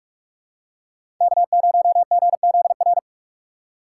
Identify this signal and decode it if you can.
Morse code, heard as a keyed tone, about 46 wpm, 700 Hz. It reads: K0GZR
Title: USA Call Signs - 6x Character Spacing - 45wpm